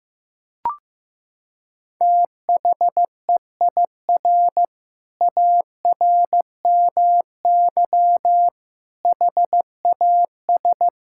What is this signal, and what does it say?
Morse code, heard as a keyed tone, about 15 wpm, 700 Hz. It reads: THEIR ARMY HAS
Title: Sentences from Top 1000 Words - Review Part 5 - 15wpm